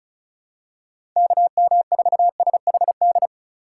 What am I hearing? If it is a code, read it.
Morse code, heard as a keyed tone, about 36 wpm, 700 Hz. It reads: KM4SHD
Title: USA Call Signs - 6 Characters - 2x Character Spacing - 35wpm